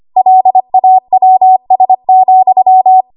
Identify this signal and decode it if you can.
Morse code, heard as a keyed tone, about 25 wpm, 750 Hz. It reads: LAWS,
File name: Archive / 25 WPM / 25WPM